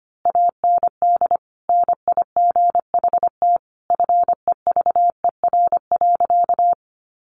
Morse code, heard as a keyed tone, about 25 wpm, 700 Hz. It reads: AND NIGHT FEVER.
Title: A Sound Of Thunder 25wpm